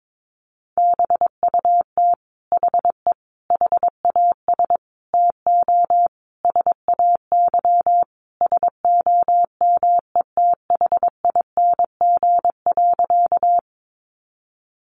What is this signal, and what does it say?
Morse code, heard as a keyed tone, about 22 wpm, 700 Hz. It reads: BUT HE HAS TO SAY SOMETHING.
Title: Sentences from Top 600 Words 22wpm